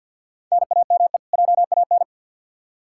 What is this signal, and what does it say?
Morse code, heard as a keyed tone, about 46 wpm, 700 Hz. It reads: NAME JAN
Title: Practice QSO - Exchange 1 - Beginning - 50wpm